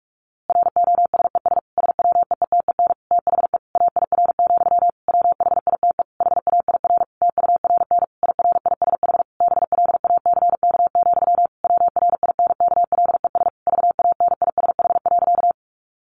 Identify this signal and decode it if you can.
Morse code, heard as a keyed tone, about 46 wpm, 700 Hz. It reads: ROSES SWEETEN THE AIR, WHITE HAIR TURN IRISH BLACK, WRINKLES VANISH.
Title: A Sound Of Thunder 45wpm